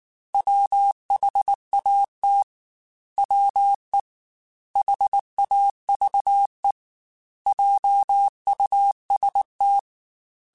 Morse code, heard as a keyed tone, about 19 wpm, 800 Hz. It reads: WHAT WE HAVE JUST
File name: Technical Subjects / Technical Subjects-mp3s / OR Gate @18